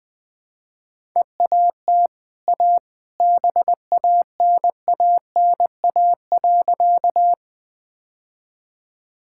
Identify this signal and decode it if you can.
Morse code, heard as a keyed tone, about 20 wpm, 700 Hz. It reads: EAT A BANANA.
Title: Easy Sentences with Hints - 20wpm